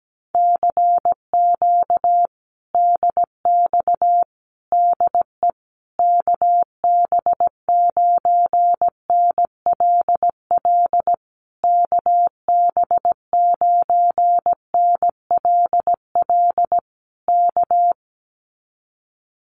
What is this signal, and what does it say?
Morse code, heard as a keyed tone, about 17 wpm, 700 Hz. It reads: CQ DX DE KB9NLL KB9NLL K
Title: Common QSO Phrases Encore 17wpm